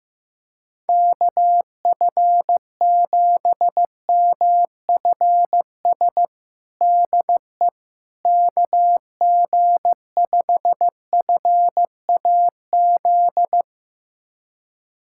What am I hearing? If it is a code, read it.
Morse code, heard as a keyed tone, about 15 wpm, 700 Hz. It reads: KF7MFS DE KG5FAZ
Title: USA Callsign DE Callsign - 6 Characters - 15wpm